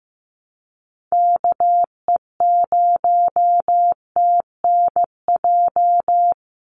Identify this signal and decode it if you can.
Morse code, heard as a keyed tone, about 15 wpm, 700 Hz. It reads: KE0TNJ
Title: USA Call Signs 15wpm